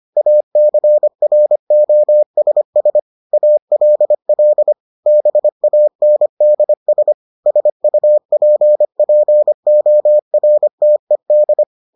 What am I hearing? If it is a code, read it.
Morse code, heard as a keyed tone, about 25 wpm, 600 Hz. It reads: ACROSS ALL BANDS SUPPORTED